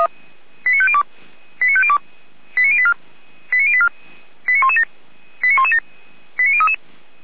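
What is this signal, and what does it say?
Five-tone selective call (ZVEI), heard as quick runs of five tones, tones of about 80 ms. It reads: 89502 89502 89084 89084 89117 89117 89033
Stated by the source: ZVEI five-tone call received on SDR.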